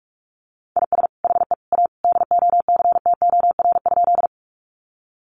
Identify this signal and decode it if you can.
Morse code, heard as a keyed tone, about 44 wpm, 700 Hz. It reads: IS HE A DOCTOR?
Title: Sentences from Top 1500 Words 45wpm